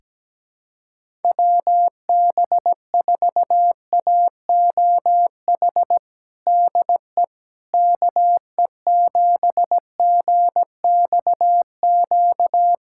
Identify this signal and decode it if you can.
Morse code, heard as a keyed tone, about 17 wpm, 700 Hz. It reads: WB4AOH DE KE7GXQ
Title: USA Callsign DE Callsign 17wpm